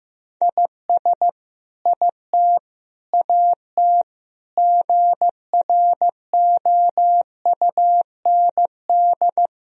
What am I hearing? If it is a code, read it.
Morse code, heard as a keyed tone, about 15 wpm, 700 Hz. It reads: IS IT AT GROUND
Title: Sentences from Top 1000 Words - Review Part 4 - 15wpm